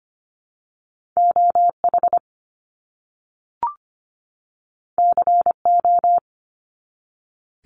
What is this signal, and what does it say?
Morse code, heard as a keyed tone, about 25 wpm, 700 Hz. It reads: OH CO
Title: US States 25wpm